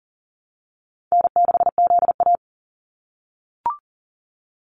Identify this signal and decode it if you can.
Morse code, heard as a keyed tone, about 40 wpm, 700 Hz. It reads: N6ZA
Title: NA SOTA Activator Callsigns 40wpm